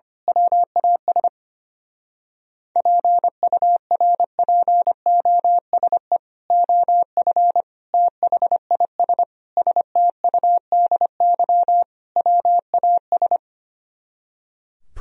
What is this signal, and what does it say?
Morse code, heard as a keyed tone, about 24 wpm, 700 Hz. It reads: WAS PURPOSE OF THIS STUDY WAS
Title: Speed-Racing - Sets of 5 Words 25wpm